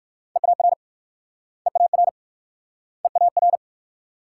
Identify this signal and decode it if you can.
Morse code, heard as a keyed tone, about 58 wpm, 690 Hz. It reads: EAR EAR EAR